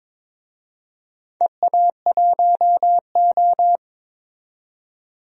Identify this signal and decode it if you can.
Morse code, heard as a keyed tone, about 22 wpm, 700 Hz. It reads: EA1O
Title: DXCC-100 Call Signs - 4 Characters - 22wpm